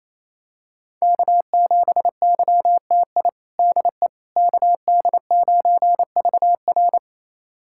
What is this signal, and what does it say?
Morse code, heard as a keyed tone, about 28 wpm, 700 Hz. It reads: K7YTI DE KD9VR
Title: USA Callsign DE Callsign - 5 Characters - 28wpm